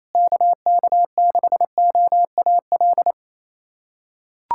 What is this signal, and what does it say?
Morse code, heard as a keyed tone, about 28 wpm, 700 Hz. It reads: KK6OAL
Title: USA Callsign DE Callsign - 6 Characters - 28wpm